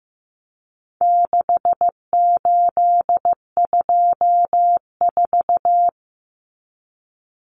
Morse code, heard as a keyed tone, about 15 wpm, 700 Hz. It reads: 6824
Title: Common Cut Numbers - 4-Digit 15wpm